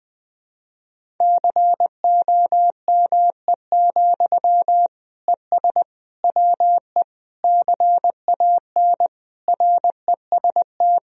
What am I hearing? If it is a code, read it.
Morse code, heard as a keyed tone, about 20 wpm, 700 Hz. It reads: COME, ES WE CAN REST
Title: Sentences from Top 500 Words and Abbreviations 20wpm